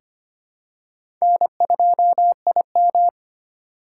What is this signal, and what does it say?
Morse code, heard as a keyed tone, about 25 wpm, 700 Hz. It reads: N2IM
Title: USA Call Signs - 4 Characters - 4x Character Spacing - 25wpm